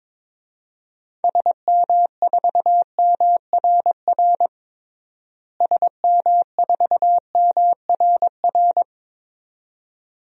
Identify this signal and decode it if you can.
Morse code, heard as a keyed tone, about 23 wpm, 700 Hz. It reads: SM4MRR SM4MRR
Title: DXCC-100 Call Signs - 6 Characters - Repeated 2x - 22wpm